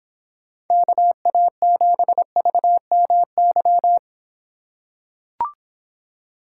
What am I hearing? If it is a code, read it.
Morse code, heard as a keyed tone, about 26 wpm, 700 Hz. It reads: KA7VMY